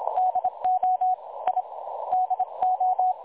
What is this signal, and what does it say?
Morse code, heard as a keyed tone, about 26 wpm, 740 Hz. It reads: DO I DO
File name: dlazaawansowanych / nagraniazpasma / websdr recording 3565.0kHz